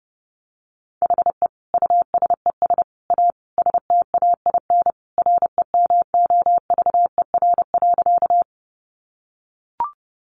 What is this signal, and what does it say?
Morse code, heard as a keyed tone, about 30 wpm, 700 Hz. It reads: HE USES A STAIN REMOVER.
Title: Additional Easy Phrases 30wpm